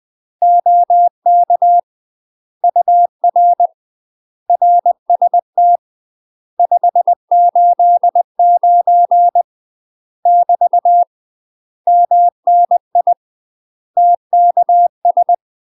Morse code, heard as a keyed tone, about 20 wpm, 700 Hz. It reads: OK UR RST 589 <BT> MNI TKS
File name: UK MASTER QSOs TRAINING / 20WPM / UK MASTER QSOs 0010